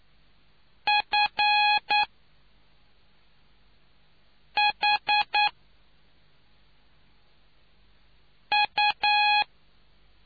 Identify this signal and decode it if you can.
Morse code, heard as a keyed tone, about 9 wpm, 830 Hz. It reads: F H U